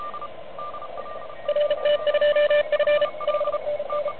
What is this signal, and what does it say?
Morse code, heard as a keyed tone, about 35 wpm, 590 Hz, with radio noise. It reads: RT2F 5NN
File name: truSDX